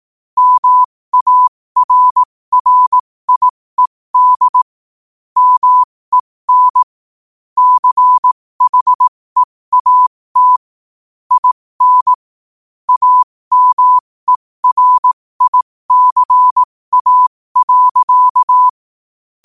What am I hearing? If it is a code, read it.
Morse code, heard as a keyed tone, about 18 wpm, spaced as for 15 wpm, 1000 Hz. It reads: MARRIED MEN CHEAT IN AMERICA.
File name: qotdfiles / 15wpm